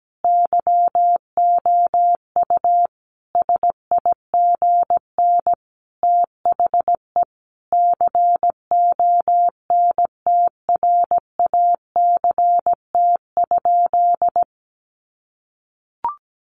Morse code, heard as a keyed tone, about 17 wpm, 700 Hz. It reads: YOU SIGN THE CONTRACT?
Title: Sentences from Top 1400 Words 17wpm